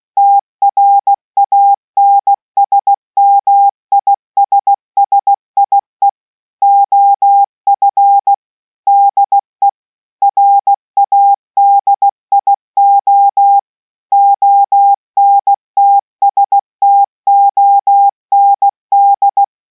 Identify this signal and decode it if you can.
Morse code, heard as a keyed tone, about 16 wpm, 800 Hz. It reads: TRANSMISSIE OF DE RADIO ONTSTOND